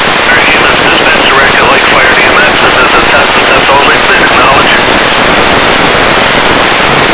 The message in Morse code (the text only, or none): none